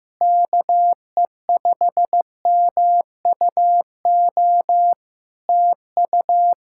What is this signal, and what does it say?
Morse code, heard as a keyed tone, about 15 wpm, 700 Hz. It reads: KE5MUO TU
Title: Practice QSO - Exchange 4 - Ending - 15wpm